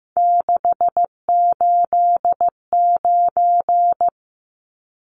Morse code, heard as a keyed tone, about 15 wpm, 700 Hz. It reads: 689
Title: Numbers - 3-Digit 15wpm